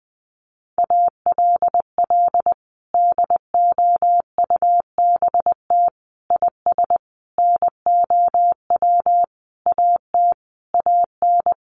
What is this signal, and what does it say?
Morse code, heard as a keyed tone, about 20 wpm, 700 Hz. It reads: ALL DOUBT IS NOW AT AN
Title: Sentences from Top 1500 Words 20wpm